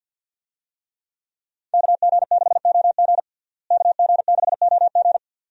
Morse code, heard as a keyed tone, about 45 wpm, 700 Hz. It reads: KG6OG KG6OG
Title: USA Call Signs - 5 Characters - Repeated 2x - 50wpm